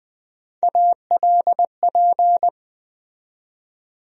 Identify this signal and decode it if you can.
Morse code, heard as a keyed tone, about 20 wpm, 700 Hz. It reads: ALP
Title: USA Call Sign Postfixes - 3 Letters 20wpm